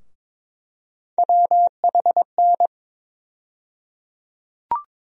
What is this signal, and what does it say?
Morse code, heard as a keyed tone, about 22 wpm, 700 Hz. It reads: WHN
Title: USA Call Sign Postfixes - 3 Letters 22wpm